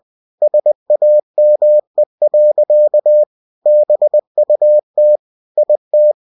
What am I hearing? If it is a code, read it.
Morse code, heard as a keyed tone, about 20 wpm, 600 Hz. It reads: SAME. BUT IT
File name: cw-books / winnie-the-pooh / winnie-20wpm-page-0107